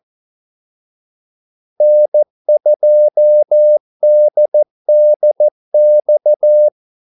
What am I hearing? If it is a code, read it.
Morse code, heard as a keyed tone, about 14 wpm, 600 Hz. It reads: N2DDX
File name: Callsigns-14wpm